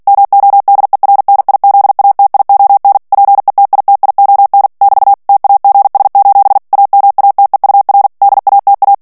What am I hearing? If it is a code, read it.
Morse code, heard as a keyed tone, about 48 wpm, 800 Hz. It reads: MODERNIZATION PETITION <BT> TAGS: AMATEUR DATA
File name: ANGLAIS texte clair 3 vitesse 45 mots minute